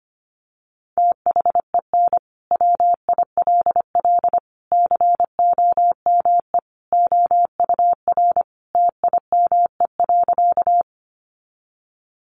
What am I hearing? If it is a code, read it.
Morse code, heard as a keyed tone, about 25 wpm, 700 Hz. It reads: THEN WILL COME OUR TIME.
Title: Sentences from Top 200 Words 25wpm